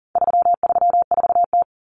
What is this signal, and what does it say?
Morse code, heard as a keyed tone, about 40 wpm, 700 Hz. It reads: 334T
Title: Common Cut Numbers - 4-Digit 40wpm